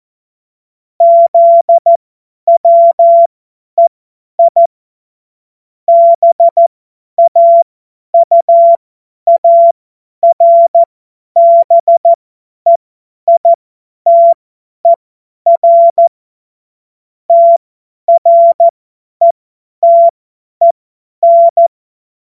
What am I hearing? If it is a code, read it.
Morse code, heard as a keyed tone, about 14 wpm, spaced as for 10 wpm, 680 Hz. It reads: ZWEI BAUARBEITER TRETEN
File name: Morsekurs / Witze / Witz 05 Dezember 2023 50 F70